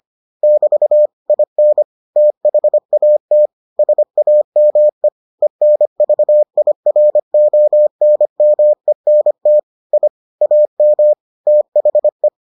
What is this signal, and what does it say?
Morse code, heard as a keyed tone, about 25 wpm, 600 Hz. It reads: <BT> IN THAT SAME ENVIRONMENT I AM THE